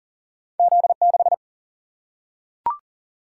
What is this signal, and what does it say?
Morse code, heard as a keyed tone, about 40 wpm, 700 Hz. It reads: Z 6
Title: Single-Letter Single-Number 40wpm